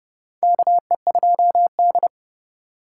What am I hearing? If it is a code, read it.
Morse code, heard as a keyed tone, about 30 wpm, 700 Hz. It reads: KE2D
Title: K1USN SST QSO Practice - 30wpm